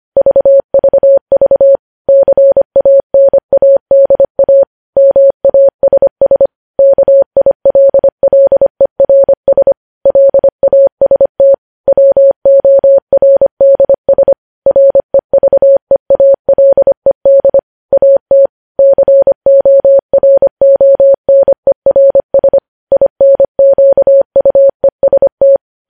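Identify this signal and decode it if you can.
Morse code, heard as a keyed tone, about 25 wpm, 550 Hz. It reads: VVV CANADA MASS KILLERS LAST WORDS REVEALED AT CORONERS INQUEST
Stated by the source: Morse code news headlines sent at 25 wpm.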